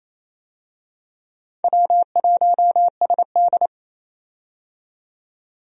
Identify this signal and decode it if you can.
Morse code, heard as a keyed tone, about 28 wpm, 700 Hz. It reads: W1SD
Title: USA Call Signs - 4 Characters - 28wpm